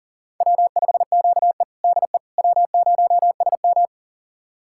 Morse code, heard as a keyed tone, about 40 wpm, 700 Hz. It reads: W5QE DE W0SM
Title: USA Callsign DE Callsign - 4 Characters - 40wpm